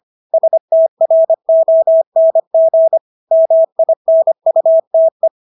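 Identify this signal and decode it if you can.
Morse code, heard as a keyed tone, about 25 wpm, 650 Hz. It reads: STRONG MINUTE
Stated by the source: Morse practice of English words at 25 wpm from CW med Gnister.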